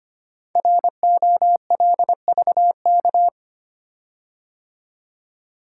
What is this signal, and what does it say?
Morse code, heard as a keyed tone, about 25 wpm, 700 Hz. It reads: ROLVK
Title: Code Groups - 5 Letter:Numbers - 25wpm